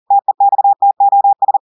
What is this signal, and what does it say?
Morse code, heard as a keyed tone, about 40 wpm, 800 Hz. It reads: TEXTOS
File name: audios / portugais / portugais texte en clair 1 vitesse 40 mots minutes